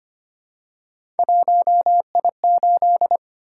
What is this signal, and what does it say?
Morse code, heard as a keyed tone, about 25 wpm, 700 Hz. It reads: 1I8
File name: Three Letter-Number Combinations - 25wpm